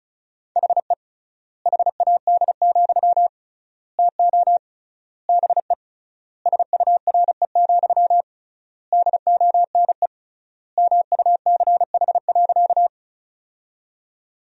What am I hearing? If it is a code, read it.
Morse code, heard as a keyed tone, about 34 wpm, 700 Hz. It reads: HE HAD, TO BE SURE, DONE MUCH.
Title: 3x Word-Spacing - Sentences from Top 500 Words - 35wpm